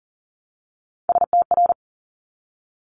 Morse code, heard as a keyed tone, about 40 wpm, 690 Hz. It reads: STR
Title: Top Three Letter Combinations in Words 40wpm